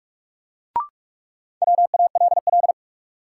Morse code, heard as a keyed tone, about 42 wpm, 700 Hz. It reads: WALL
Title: Sentences from Top 1100 Words - New Word Drill Alt - 45wpm